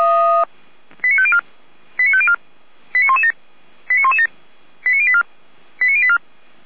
Five-tone selective call (ZVEI), heard as quick runs of five tones, tones of about 70 ms. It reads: 89504 89504 89117 89117 89084 89084
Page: SDR/ZVEI